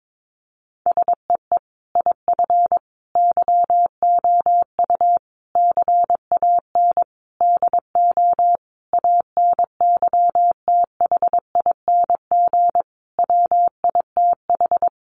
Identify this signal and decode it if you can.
Morse code, heard as a keyed tone, about 22 wpm, 700 Hz. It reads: SEE IF YOU CAN DO ANYTHING WITH
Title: Sentences from Top 1000 Words 22wpm